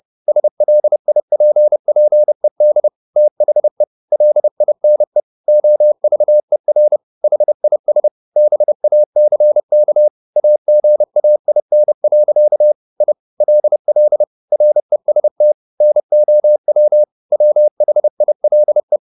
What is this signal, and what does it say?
Morse code, heard as a keyed tone, about 30 wpm, 600 Hz. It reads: SLIPPED THE LINE OVER HIS BACK AGAIN. I LL REST NOW WHILE